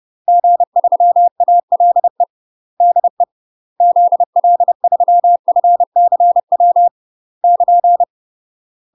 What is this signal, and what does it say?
Morse code, heard as a keyed tone, about 30 wpm, 700 Hz. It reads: G3ALE DE ZL3FCW <KN>
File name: UK MASTER QSOs TRAINING / 30WPM / UK MASTER QSOs 0020